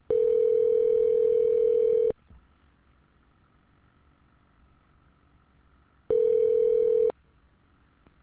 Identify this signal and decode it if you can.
Ringback tone (North America), 440 + 480 Hz.